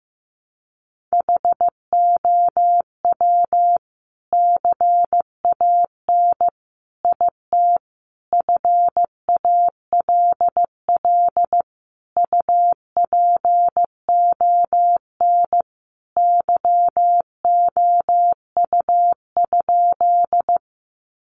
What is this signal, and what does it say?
Morse code, heard as a keyed tone, about 15 wpm, 700 Hz. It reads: HOW CAN IT FALL UPON YOU?
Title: Sentences from Top 1100 Words 15wpm